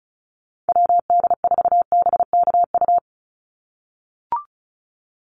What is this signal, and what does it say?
Morse code, heard as a keyed tone, about 34 wpm, 700 Hz. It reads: WD4BKU
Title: USA Call Signs 35wpm